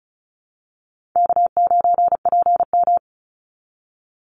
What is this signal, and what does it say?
Morse code, heard as a keyed tone, about 34 wpm, 700 Hz. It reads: K9PM
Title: NA SOTA Activator Callsigns 35wpm